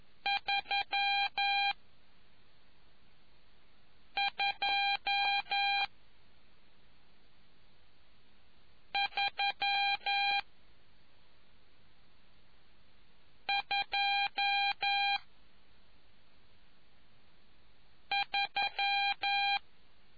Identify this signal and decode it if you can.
Morse code, heard as a keyed tone, about 11 wpm, spaced as for 3 wpm, 800 Hz. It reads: 32323